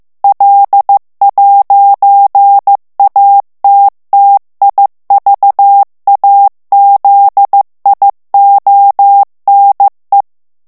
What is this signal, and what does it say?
Morse code, heard as a keyed tone, about 15 wpm, 800 Hz. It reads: L'ATTIVAZIONE
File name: ITALIEN texte clair 1 vitesse 14 mots minute